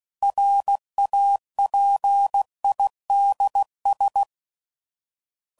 Morse code, heard as a keyed tone, about 16 wpm, 800 Hz. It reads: RAPIDS
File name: CW Code Groups @15WPM